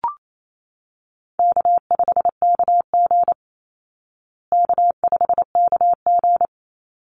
Morse code, heard as a keyed tone, about 28 wpm, 700 Hz. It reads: K5KG K5KG
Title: Easy Call Signs Repeated 3x - 28wpm